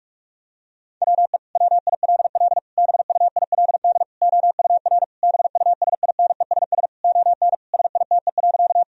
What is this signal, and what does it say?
Morse code, heard as a keyed tone, about 46 wpm, 700 Hz. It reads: WE WILL BUILD OUR BUSINESS ON SITE.
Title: Sentences from Top 1000 Words - Review Part 4 - 45wpm